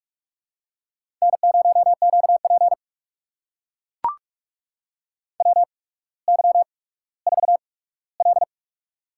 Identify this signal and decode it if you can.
Morse code, heard as a keyed tone, about 44 wpm, spaced as for 19 wpm, 700 Hz. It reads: N0QP WY4L
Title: USA Call Signs - 4 Characters - 8x Character Spacing - 45wpm